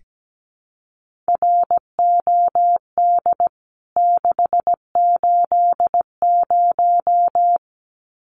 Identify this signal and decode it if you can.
Morse code, heard as a keyed tone, about 17 wpm, 700 Hz. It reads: ROD 680
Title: CWT Exchange 17wpm